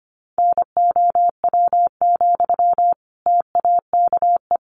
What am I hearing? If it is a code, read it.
Morse code, heard as a keyed tone, about 25 wpm, 700 Hz. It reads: NOW, TAKE
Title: Sentences from Top 400 Words 25wpm